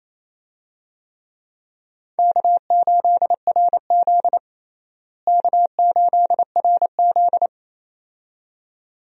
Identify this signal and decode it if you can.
Morse code, heard as a keyed tone, about 28 wpm, 700 Hz. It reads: K8RZ K8RZ
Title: USA Call Signs - 4 Characters - Repeated 2x - 28wpm